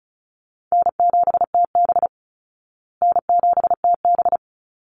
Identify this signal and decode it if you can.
Morse code, heard as a keyed tone, about 34 wpm, 700 Hz. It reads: N7TB N7TB
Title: Easy Call Signs Repeated 3x - 35wpm